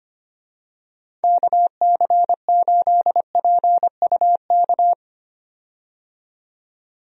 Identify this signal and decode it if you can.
Morse code, heard as a keyed tone, about 25 wpm, 700 Hz. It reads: KC8PUK
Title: USA Call Signs - 6 Characters - 25wpm